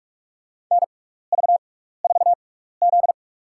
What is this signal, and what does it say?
Morse code, heard as a keyed tone, about 46 wpm, spaced as for 16 wpm, 700 Hz. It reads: NV4Z
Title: USA Call Signs - 4 Characters - 6x Character Spacing - 45wpm